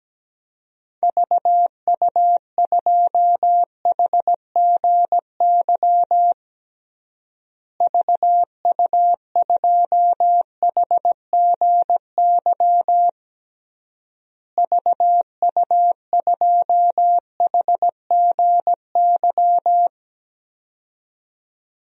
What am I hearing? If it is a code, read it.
Morse code, heard as a keyed tone, about 17 wpm, 700 Hz. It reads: VU2HGY VU2HGY VU2HGY
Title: DXCC-100 Call Signs - 6 Characters - Repeated 3x - 17wpm